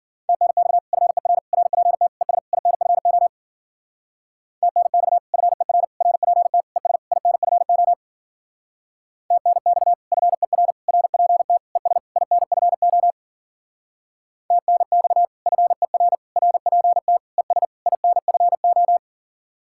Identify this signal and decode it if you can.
Morse code, heard as a keyed tone, about 51 wpm, 700 Hz. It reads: TNX FER RPT ES INFO TNX FER RPT ES INFO TNX FER RPT ES INFO TNX FER RPT ES INFO